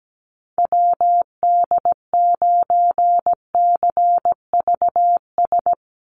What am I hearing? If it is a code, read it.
Morse code, heard as a keyed tone, about 17 wpm, 700 Hz. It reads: WD9CVS
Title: USA Call Signs 17wpm